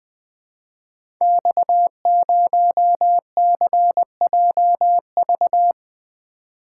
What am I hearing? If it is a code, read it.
Morse code, heard as a keyed tone, about 20 wpm, 700 Hz. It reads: X0CJV
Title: Code Groups - 5 Letter:Numbers - 20wpm